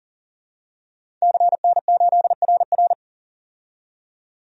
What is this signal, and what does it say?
Morse code, heard as a keyed tone, about 40 wpm, 700 Hz. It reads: CN8RR
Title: DXCC-100 Call Signs - 5 Characters - 40wpm